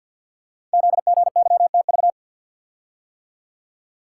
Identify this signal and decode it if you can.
Morse code, heard as a keyed tone, about 52 wpm, 700 Hz. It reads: ZGYTV
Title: Code Groups - 5 Letters - 50wpm